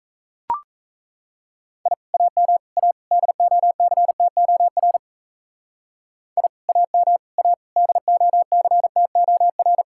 Morse code, heard as a keyed tone, about 33 wpm, 700 Hz. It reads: I AM A DOCTOR I AM A DOCTOR
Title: Speed-Racing Common QSO Phrases - 28wpm